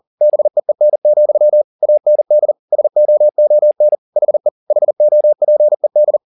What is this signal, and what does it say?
Morse code, heard as a keyed tone, about 40 wpm, 600 Hz. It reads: BEEN, AND SOON HE HOPED
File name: the old man and the sea 2 40wpm